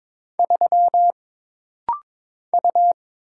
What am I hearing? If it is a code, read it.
Morse code, heard as a keyed tone, about 22 wpm, 700 Hz. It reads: 3 U
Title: Single Letter-Number Alt - Rapid-Fire 22wpm